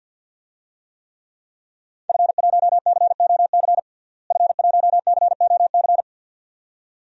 Morse code, heard as a keyed tone, about 53 wpm, 700 Hz. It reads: F1COC F1COC
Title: DXCC-100 Call Signs - 5 Characters - Repeated 2x - 50wpm